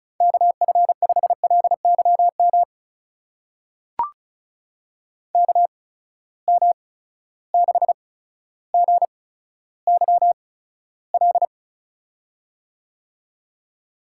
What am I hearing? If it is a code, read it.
Morse code, heard as a keyed tone, about 36 wpm, spaced as for 15 wpm, 700 Hz. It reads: KF5LYM KM6GYL
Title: USA Call Signs - 6 Characters - 8x Character Spacing - 35wpm